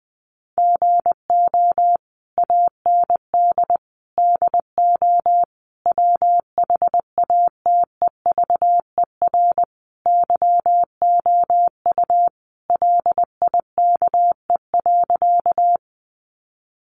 Morse code, heard as a keyed tone, about 20 wpm, 700 Hz. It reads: GO AND DO WHATEVER YOU LIKE.
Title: Sentences from Top 1100 Words 20wpm